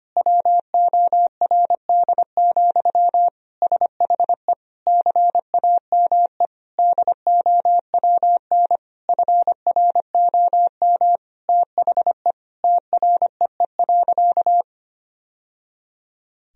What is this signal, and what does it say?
Morse code, heard as a keyed tone, about 25 wpm, 700 Hz. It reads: WORD, SHE CAME DOWN FROM THE TREE.
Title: Sentences from Top 1000 Words - Review Part 1 - 25wpm